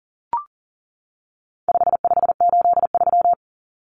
Morse code, heard as a keyed tone, about 40 wpm, 700 Hz. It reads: 5583
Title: Common Cut Numbers - 4-Digit 40wpm